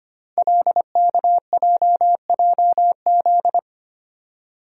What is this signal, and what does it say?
Morse code, heard as a keyed tone, about 25 wpm, 690 Hz. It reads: LKJJZ
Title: Code Groups - 5 Letters - 25wpm